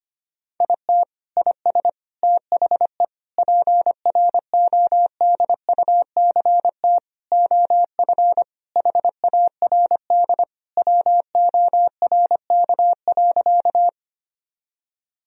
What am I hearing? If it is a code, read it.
Morse code, heard as a keyed tone, about 25 wpm, 700 Hz. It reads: IT IS THE PRODUCT OF HARD WORK.
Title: Sentences from Top 1000 Words - Review Part 2 - 25wpm